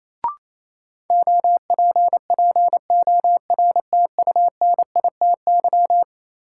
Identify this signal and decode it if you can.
Morse code, heard as a keyed tone, about 28 wpm, 700 Hz. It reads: OPPORTUNITY
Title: Sentences from Top 1000 Words - New Word Drill - 28wpm